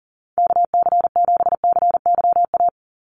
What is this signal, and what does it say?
Morse code, heard as a keyed tone, about 40 wpm, 700 Hz. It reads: KC7CYA
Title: USA Call Signs 40wpm